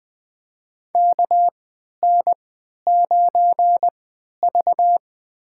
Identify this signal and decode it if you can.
Morse code, heard as a keyed tone, about 20 wpm, spaced as for 11 wpm, 700 Hz. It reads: KN9V
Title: USA Call Signs - 3x Character Spacing - 20wpm